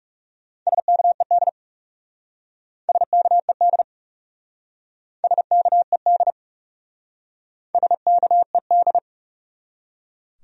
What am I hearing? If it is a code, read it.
Morse code, heard as a keyed tone, about 37 wpm, 700 Hz. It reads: SKED SKED SKED SKED
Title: Speed-Racing Abbreviations - 30wpm